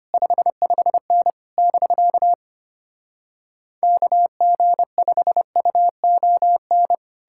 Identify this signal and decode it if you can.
Morse code, heard as a keyed tone, about 33 wpm, 700 Hz. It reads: 55N <BK> KG5UON